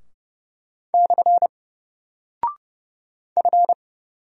Morse code, heard as a keyed tone, about 30 wpm, 700 Hz. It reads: / F
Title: Single Character (Letter:Number:Punctuation:Prosign) Alt 30wpm